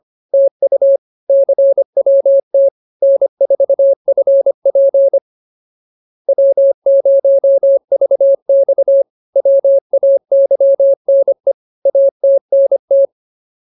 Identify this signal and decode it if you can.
Morse code, heard as a keyed tone, about 25 wpm, 550 Hz. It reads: TU CWT N4FP W0VX WAYNE ATNT